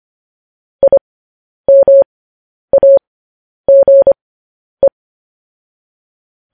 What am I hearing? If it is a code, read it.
Morse code, heard as a keyed tone, about 25 wpm, spaced as for 10 wpm, 550 Hz. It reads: IMAGE
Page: News Headlines in Morse Code at 10 WPM